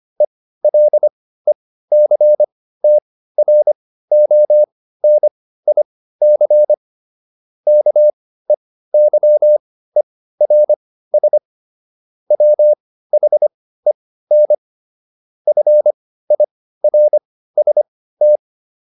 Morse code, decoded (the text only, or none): ELECTRONIC KEYERS WHEN FIRST